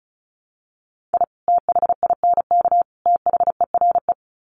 Morse code, heard as a keyed tone, about 34 wpm, 700 Hz. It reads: I THINK THERE
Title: Sets of 3 Words 35wpm